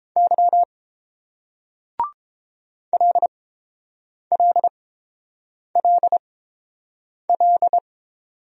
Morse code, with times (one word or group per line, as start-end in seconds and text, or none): 0.2-0.6 Y
2.9-7.8 LLLL